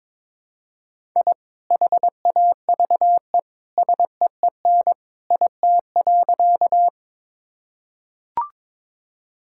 Morse code, heard as a keyed tone, about 22 wpm, 700 Hz. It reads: I HAVE SEEN IT.
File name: practice / 22wpm / Sentences with 2-Word Phrases - 22wpm